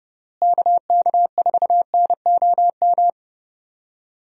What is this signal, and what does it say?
Morse code, heard as a keyed tone, about 30 wpm, 700 Hz. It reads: KK4NOM
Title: USA Call Signs - 6 Characters - Repeated 3x - 30wpm